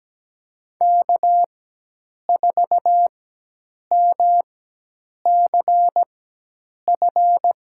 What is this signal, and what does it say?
Morse code, heard as a keyed tone, about 17 wpm, spaced as for 8 wpm, 700 Hz. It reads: K4MCF
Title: USA Call Signs - 5 Characters - 4x Character Spacing - 17wpm